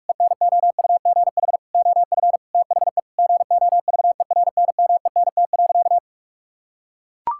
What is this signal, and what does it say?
Morse code, heard as a keyed tone, about 41 wpm, 700 Hz. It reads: ENOUGH OF THE GOVERNMENT.
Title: Sentences from Top 1600 Words 45wpm